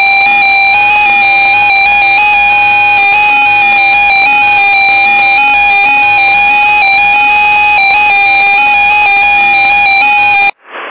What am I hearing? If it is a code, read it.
FT8.
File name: contests / 2019-june=vhf-k5tr- -0101 0-2